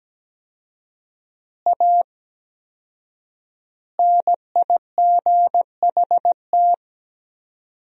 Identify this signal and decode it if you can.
Morse code, heard as a keyed tone, about 17 wpm, 700 Hz. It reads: A NIGHT
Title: 4x Word-Spacing - Sentences from Top 300 Words - 17wpm